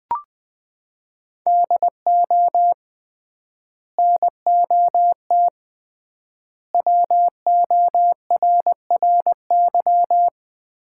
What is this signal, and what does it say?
Morse code, heard as a keyed tone, about 20 wpm, 700 Hz. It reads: DO NOT WORRY
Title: 3x Word-Spacing - Sets of 5 Words - 20wpm